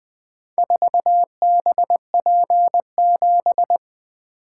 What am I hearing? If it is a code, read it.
Morse code, heard as a keyed tone, about 20 wpm, 700 Hz. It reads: 4BP7
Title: Four Letter-Number Combinations - 20wpm